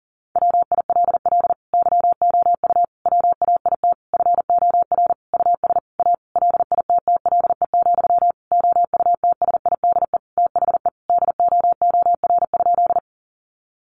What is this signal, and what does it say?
Morse code, heard as a keyed tone, about 40 wpm, 700 Hz. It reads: WILL YOU WAIT FOR US A LITTLE, OUTSIDE THE DOOR?
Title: Sentences from Top 900 Words 40wpm